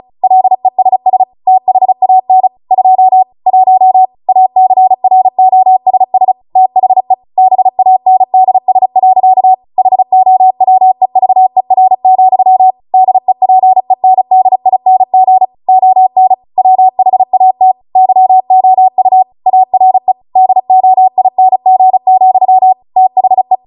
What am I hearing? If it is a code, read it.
Morse code, heard as a keyed tone, about 36 wpm, 750 Hz. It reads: LESS THAN 2 1 ACROSS THE BANDS. HOWEVER, DEPENDING ON WHAT YOU ARE DOING, THE